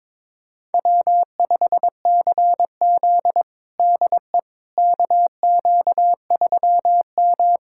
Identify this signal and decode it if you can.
Morse code, heard as a keyed tone, about 22 wpm, 700 Hz. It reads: W5CZ DE KQ3M
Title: USA Callsign DE Callsign - 4 Characters - 22wpm